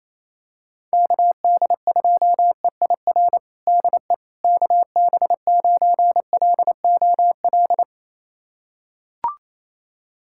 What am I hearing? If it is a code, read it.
Morse code, heard as a keyed tone, about 28 wpm, 700 Hz. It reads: KD2EIR DE KB9LOL
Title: USA Callsign DE Callsign - 6 Characters - 28wpm